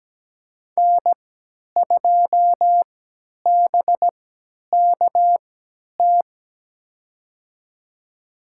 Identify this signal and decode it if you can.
Morse code, heard as a keyed tone, about 17 wpm, spaced as for 10 wpm, 700 Hz. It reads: N2BKT
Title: USA Call Signs - 5 Characters - 3x Character Spacing - 17wpm